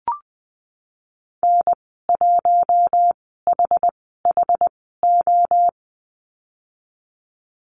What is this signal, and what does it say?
Morse code, heard as a keyed tone, about 20 wpm, spaced as for 14 wpm, 700 Hz. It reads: N1HHO
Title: 062 Lesson 11g - Letter h - Call Signs 20wpm